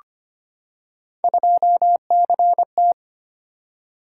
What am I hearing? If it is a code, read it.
Morse code, heard as a keyed tone, about 25 wpm, 700 Hz. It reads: 2CT
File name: Three Letter-Number Combinations - 25wpm